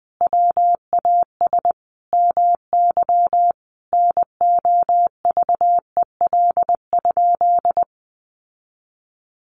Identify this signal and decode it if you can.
Morse code, heard as a keyed tone, about 20 wpm, 700 Hz. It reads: WAS MY NOVEL?
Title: Sentences from Top 1300 Words 20wpm